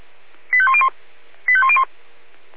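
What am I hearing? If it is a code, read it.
Five-tone selective call (ZVEI), heard as quick runs of five tones, tones of about 70 ms. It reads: 85101 85101
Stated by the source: ZVEI five-tone selective call received on SDR.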